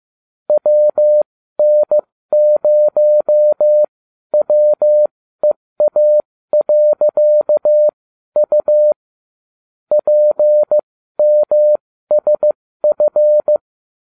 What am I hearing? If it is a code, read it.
Morse code, heard as a keyed tone, about 15 wpm, spaced as for 12 wpm, 600 Hz. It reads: WN0WEA.U PMSF